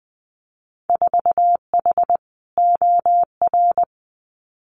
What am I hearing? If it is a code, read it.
Morse code, heard as a keyed tone, about 20 wpm, 700 Hz. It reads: 4H OR
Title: Winter Field Day Exchanges Encore 20wpm